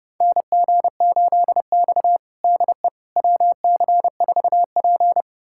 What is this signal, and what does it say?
Morse code, heard as a keyed tone, about 31 wpm, 700 Hz. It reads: NG8X DE WC4P
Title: USA Callsign DE Callsign - 4 Characters - 30wpm